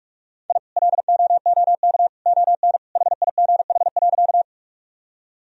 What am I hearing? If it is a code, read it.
Morse code, heard as a keyed tone, about 48 wpm, 700 Hz. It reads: I LOOK ON HIGH.
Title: Sentences from Top 1000 Words - Review Part 1 - 45wpm